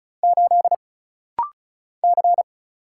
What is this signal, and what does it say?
Morse code, heard as a keyed tone, about 35 wpm, 700 Hz. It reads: 8 C
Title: Single Letter-Number - Rapid-Fire 35wpm